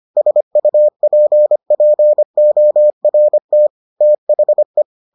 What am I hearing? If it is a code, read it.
Morse code, heard as a keyed tone, about 25 wpm, 600 Hz. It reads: SUPPORT THE